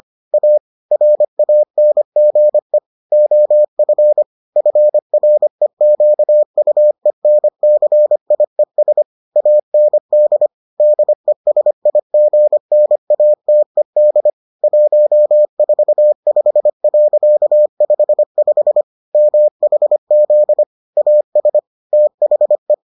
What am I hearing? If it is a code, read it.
Morse code, heard as a keyed tone, about 25 wpm, 600 Hz. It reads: A RANGE OF FREQUENCIES AND DESIGNATED 145.55 MHZ AS THE